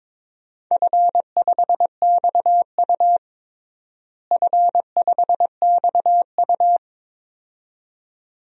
Morse code, heard as a keyed tone, about 22 wpm, 700 Hz. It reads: F5XU F5XU
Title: DXCC-100 Call Signs - 4 Characters - Repeated 2x - 22wpm